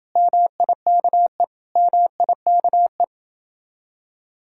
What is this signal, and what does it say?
Morse code, heard as a keyed tone, about 30 wpm, 700 Hz. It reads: MIKE MIKE